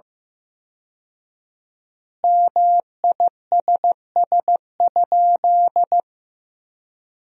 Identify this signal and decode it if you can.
Morse code, heard as a keyed tone, about 15 wpm, 700 Hz. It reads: MISS?
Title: 4x Word-Spacing - Sentences from Top 500 Words - 15wpm